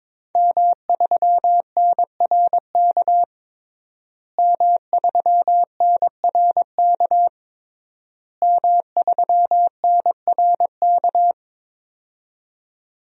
Morse code, heard as a keyed tone, about 22 wpm, 700 Hz. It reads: M3NRK M3NRK M3NRK
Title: DXCC-100 Call Signs - 5 Characters - Repeated 3x - 22wpm